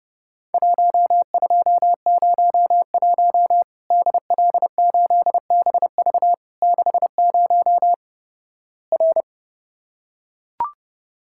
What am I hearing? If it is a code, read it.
Morse code, heard as a keyed tone, about 31 wpm, 700 Hz. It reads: 1201 DL8BV 60 R
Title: Worked All Europe DX Contest - QTC Exchange - 30wpm